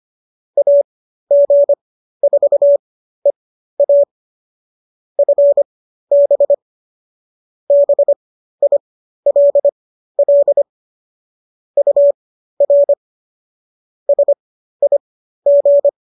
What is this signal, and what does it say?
Morse code, heard as a keyed tone, about 25 wpm, spaced as for 13 wpm, 570 Hz. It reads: AG4EA FB BILL UR SIG